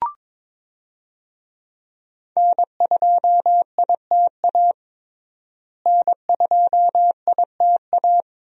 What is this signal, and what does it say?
Morse code, heard as a keyed tone, about 22 wpm, 700 Hz. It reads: N2ITA N2ITA
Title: USA Call Signs - 5 Characters - Repeated 3x - 22wpm